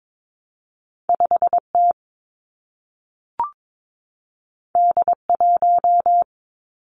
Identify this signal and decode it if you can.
Morse code, heard as a keyed tone, about 22 wpm, 700 Hz. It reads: 5 T D 1
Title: Two Challenging Letter-Number Combinations 22wpm